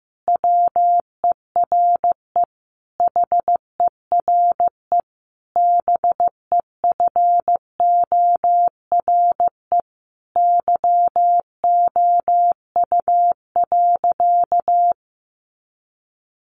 Morse code, heard as a keyed tone, about 14 wpm, 700 Hz. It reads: WERE HERE BEFORE YOU.
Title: Sentences from Top 1400 Words 15wpm